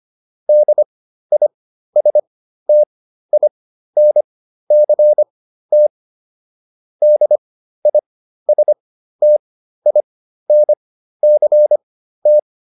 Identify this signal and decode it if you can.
Morse code, heard as a keyed tone, about 25 wpm, spaced as for 13 wpm, 600 Hz. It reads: DISTINCT DISTINCT